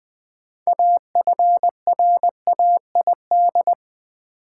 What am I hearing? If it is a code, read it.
Morse code, heard as a keyed tone, about 20 wpm, 700 Hz. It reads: AFRAID
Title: Sentences from Top 1500 Words - New Word Drill - 20wpm